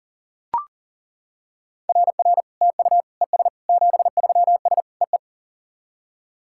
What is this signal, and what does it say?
Morse code, heard as a keyed tone, about 40 wpm, 690 Hz. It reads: RR TU ES 73S EE
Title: SOTA QSO Practice 40wpm